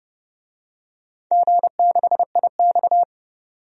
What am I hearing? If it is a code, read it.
Morse code, heard as a keyed tone, about 30 wpm, 700 Hz. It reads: G6IX
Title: DXCC-100 Call Signs - 4 Characters - 30wpm